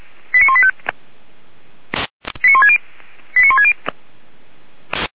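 Five-tone selective call (ZVEI), heard as quick runs of five tones, tones of about 80 ms. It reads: 89185 89160 89160
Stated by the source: Five-tone ZVEI selective call received on an SDR.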